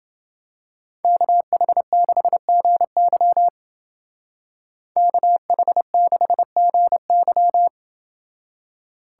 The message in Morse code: KH6GY KH6GY

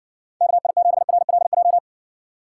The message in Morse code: KI7RLJ

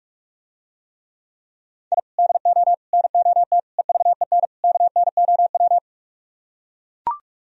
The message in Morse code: I DO NOT EVEN KNOW

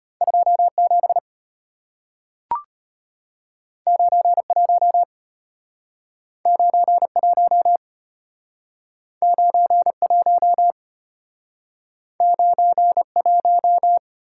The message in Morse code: 27 91 91 91 91

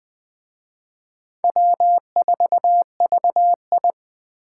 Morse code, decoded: W4VI

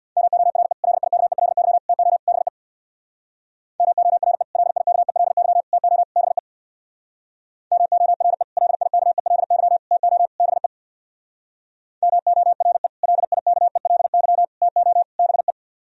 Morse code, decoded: MORE LIKELY TO BE MORE LIKELY TO BE MORE LIKELY TO BE MORE LIKELY TO BE

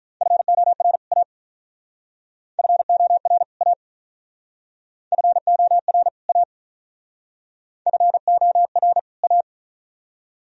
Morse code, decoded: FOR A FOR A FOR A FOR A

